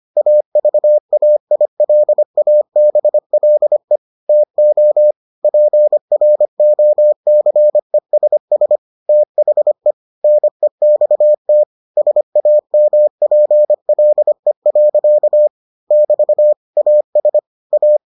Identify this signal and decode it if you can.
Morse code, heard as a keyed tone, about 25 wpm, 600 Hz. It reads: AVAILABLE TO PROCESS THE NEXT SAMPLE. <BT> AS A